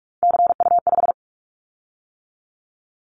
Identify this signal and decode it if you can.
Morse code, heard as a keyed tone, about 46 wpm, 700 Hz. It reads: CU5
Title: Callsign Prefixes for Top 100 DXCC - 45wpm